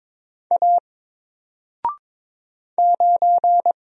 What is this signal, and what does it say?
Morse code, heard as a keyed tone, about 22 wpm, 700 Hz. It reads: A 9
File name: Single-Letter-Number-22wpm